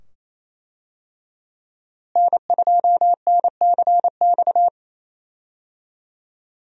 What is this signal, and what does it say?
Morse code, heard as a keyed tone, about 28 wpm, 700 Hz. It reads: N2NCX
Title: USA Call Signs - 5 Characters - Repeated 2x - 28wpm